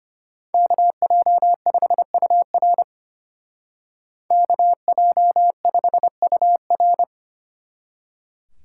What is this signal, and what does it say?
Morse code, heard as a keyed tone, about 31 wpm, 700 Hz. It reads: KJ5UR KJ5UR
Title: Speed Racing - USA 2x2 Call Signs 25wpm